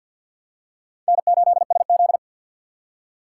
Morse code, heard as a keyed tone, about 49 wpm, 700 Hz. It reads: N8SZ
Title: USA Call Signs - 4 Characters - 6x Character Spacing - 50wpm